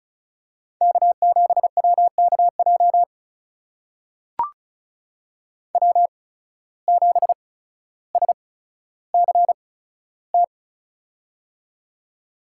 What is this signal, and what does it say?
Morse code, heard as a keyed tone, about 36 wpm, spaced as for 15 wpm, 700 Hz. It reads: K7WKJ W7SCT
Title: USA Call Signs - 5 Characters - 8x Character Spacing - 35wpm